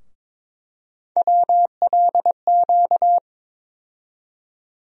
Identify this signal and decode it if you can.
Morse code, heard as a keyed tone, about 22 wpm, 700 Hz. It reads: WLQ
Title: USA Call Sign Postfixes - 3 Letters 22wpm